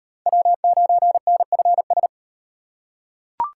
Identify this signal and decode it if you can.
Morse code, heard as a keyed tone, about 38 wpm, 700 Hz. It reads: W9NFS